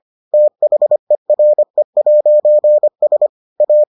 Morse code, heard as a keyed tone, about 25 wpm, 600 Hz. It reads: THERE'S A